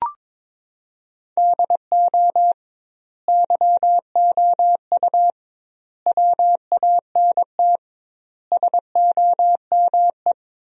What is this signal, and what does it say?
Morse code, heard as a keyed tone, about 22 wpm, 700 Hz. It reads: DO YOU WANT SOME
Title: 2x Word-Spacing - Sentences from Top 300 Words - 22wpm